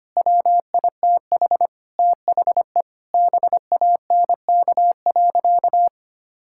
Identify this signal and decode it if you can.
Morse code, heard as a keyed tone, about 25 wpm, 700 Hz. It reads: WITH THE BANK.
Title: Sentences from Top 1000 Words - Review Part 6 - 25wpm